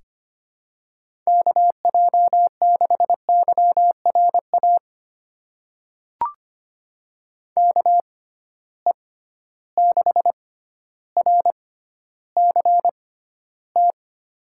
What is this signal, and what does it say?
Morse code, heard as a keyed tone, about 25 wpm, spaced as for 13 wpm, 700 Hz. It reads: KJ6YRA KE6RCT
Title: USA Call Signs - 6 Characters - 6x Character Spacing - 25wpm